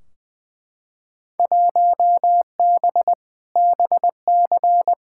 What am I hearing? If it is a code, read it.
Morse code, heard as a keyed tone, about 20 wpm, 700 Hz. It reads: 1B BC
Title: Field Day Exchanges - 20wpm